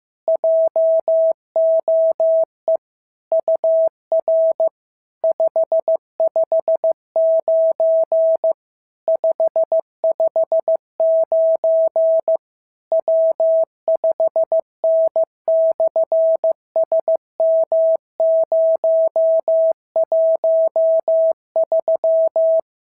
Morse code, heard as a keyed tone, about 15 wpm, 690 Hz. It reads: JOE UR 559 559 W5N/SM013
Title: SOTA QSO Practice 15wpm